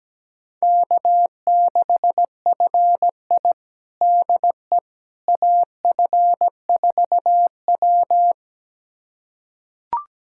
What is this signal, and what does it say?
Morse code, heard as a keyed tone, about 17 wpm, 700 Hz. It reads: K6FI DE AF4W
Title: USA Callsign DE Callsign - 4 Characters - 17wpm